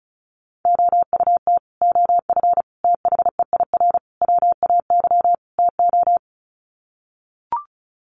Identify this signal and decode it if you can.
Morse code, heard as a keyed tone, about 34 wpm, 700 Hz. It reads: OUT OF THEIR WAY TO
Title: Sets of 5 Words Encore2 35wpm